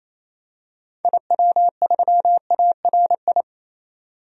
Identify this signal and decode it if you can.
Morse code, heard as a keyed tone, about 28 wpm, 700 Hz. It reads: IW3ARI
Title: DXCC-100 Call Signs - 6 Characters - 28wpm